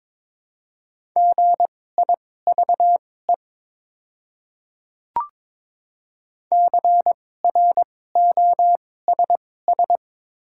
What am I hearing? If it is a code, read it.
Morse code, heard as a keyed tone, about 22 wpm, spaced as for 16 wpm, 700 Hz. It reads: GIVE CROSS
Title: 2x Character Spacing - Top 500 Words - 22wpm